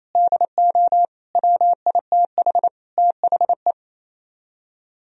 Morse code, heard as a keyed tone, about 36 wpm, 700 Hz. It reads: DO WITH THE